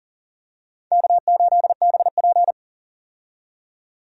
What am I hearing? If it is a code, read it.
Morse code, heard as a keyed tone, about 40 wpm, 700 Hz. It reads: K8BP
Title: USA Call Signs - 4 Characters - 40wpm